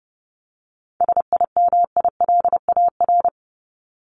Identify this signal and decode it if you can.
Morse code, heard as a keyed tone, about 30 wpm, 700 Hz. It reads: SIMILAR
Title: LY Suffix - Top 100 Words with Base Words 30wpm